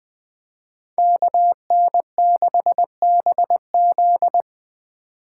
Morse code, heard as a keyed tone, about 20 wpm, 700 Hz. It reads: KN6BZ